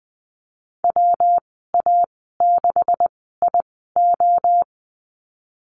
Morse code, heard as a keyed tone, about 20 wpm, spaced as for 15 wpm, 700 Hz. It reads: WA6IO
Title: 128 Lesson 21g - Number 6 - Call Signs 20wpm